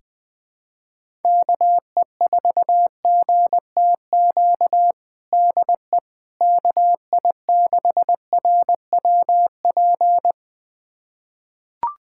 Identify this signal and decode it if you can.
Morse code, heard as a keyed tone, about 21 wpm, 700 Hz. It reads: KE4GTQ DE KI6RWP
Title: USA Callsign DE Callsign - 6 Characters - 20wpm